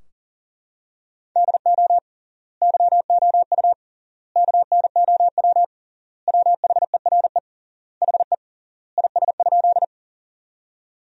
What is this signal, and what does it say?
Morse code, heard as a keyed tone, about 40 wpm, 700 Hz. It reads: DO YOU KNOW WHERE HE IS?
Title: 3x Word-Spacing - Sentences from Top 300 Words - 40wpm